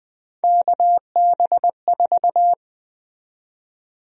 Morse code, heard as a keyed tone, about 20 wpm, 700 Hz. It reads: KB4
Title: USA Call Sign Prefixes - 20wpm